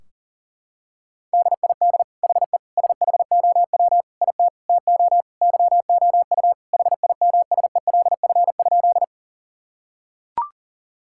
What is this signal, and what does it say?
Morse code, heard as a keyed tone, about 40 wpm, 700 Hz. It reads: DID HE SHOW IT TO YOU HIMSELF?
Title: Sentences from Top 1600 Words 40wpm